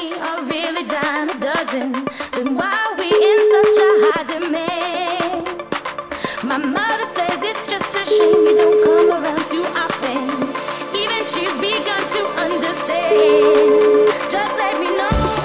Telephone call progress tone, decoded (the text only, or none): ringback tone